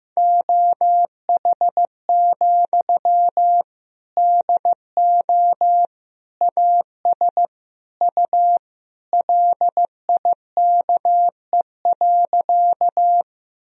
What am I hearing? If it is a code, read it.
Morse code, heard as a keyed tone, about 15 wpm, 700 Hz. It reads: OH, DO AS U LIKE.
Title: Sentences from Top 500 Words and Abbreviations 15wpm